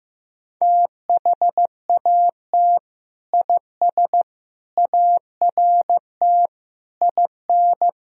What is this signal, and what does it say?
Morse code, heard as a keyed tone, about 15 wpm, 700 Hz. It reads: THAT IS ART IN
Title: Sentences from Top 1000 Words - Review Part 3 - 15wpm